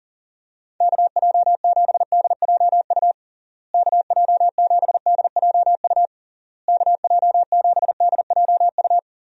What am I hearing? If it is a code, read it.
Morse code, heard as a keyed tone, about 40 wpm, 700 Hz. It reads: KJ7DJU KJ7DJU KJ7DJU
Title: USA Call Signs - 6 Characters - Repeated 3x - 40wpm